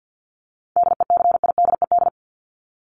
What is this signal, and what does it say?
Morse code, heard as a keyed tone, about 50 wpm, 700 Hz. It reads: DECIDED
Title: ED Suffix - Top 100 Words with Base Words 50wpm